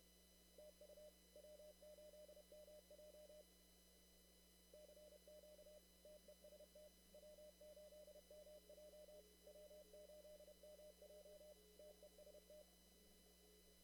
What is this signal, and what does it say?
Morse code, heard as a keyed tone, about 29 wpm, 600 Hz. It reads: TU W8MJ CQ TEST W8MJ W8MJ TEST
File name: CW / kt5j wpx cw-op-k5pi